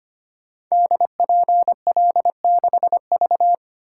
Morse code, heard as a keyed tone, about 25 wpm, 700 Hz. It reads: DPL6V
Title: Code Groups - 5 Letter:Numbers - 25wpm